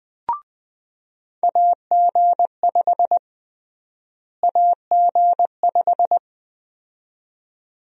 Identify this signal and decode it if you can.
Morse code, heard as a keyed tone, about 20 wpm, 700 Hz. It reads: AG5 AG5
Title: USA Call Sign Prefixes - Repeated 2x - 20wpm